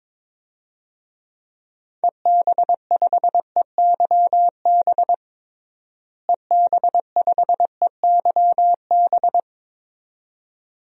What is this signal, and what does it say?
Morse code, heard as a keyed tone, about 22 wpm, 700 Hz. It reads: EB5EYB EB5EYB
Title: DXCC-100 Call Signs - 6 Characters - Repeated 2x - 22wpm